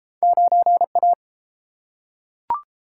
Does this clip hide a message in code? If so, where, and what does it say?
Yes: Morse code from 0.2 to 1.1 s, heard as a keyed tone, about 30 wpm, 700 Hz. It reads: 9A